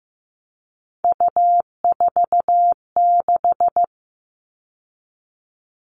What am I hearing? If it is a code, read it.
Morse code, heard as a keyed tone, about 15 wpm, 700 Hz. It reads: U46
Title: All Cut Numbers - 3-Digit 15wpm